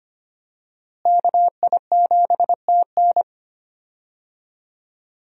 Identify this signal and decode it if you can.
Morse code, heard as a keyed tone, about 25 wpm, 700 Hz. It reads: KI7TN
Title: USA Call Signs - 5 Characters - Repeated 2x - 25wpm